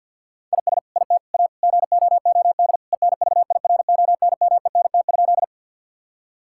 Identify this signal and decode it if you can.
Morse code, heard as a keyed tone, about 47 wpm, 700 Hz. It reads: IS IT A GOOD ENVIRONMENT?
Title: Sentences from Top 1000 Words - Review Part 5 - 50wpm